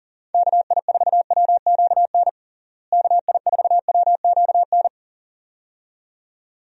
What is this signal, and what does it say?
Morse code, heard as a keyed tone, about 40 wpm, 700 Hz. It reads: KI4WQN KI4WQN
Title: USA Call Signs - 6 Characters - Repeated 2x - 40wpm